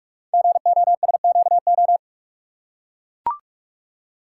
Morse code, heard as a keyed tone, about 44 wpm, 700 Hz. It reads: GOSQO
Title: Code Groups - 5 Letters - 45wpm